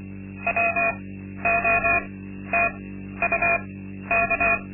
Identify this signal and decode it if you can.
Morse code, heard as a keyed tone, about 24 wpm, spaced as for 12 wpm, 680 Hz. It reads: WOTUK
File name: Linksmakalnis-2